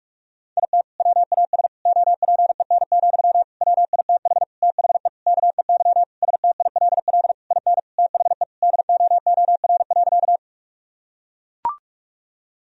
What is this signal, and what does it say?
Morse code, heard as a keyed tone, about 41 wpm, 700 Hz. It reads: IT WAS OPEN, WITH THE KEY STILL IN THE DOOR.
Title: Sentences from Top 1000 Words - Review Part 4 - 45wpm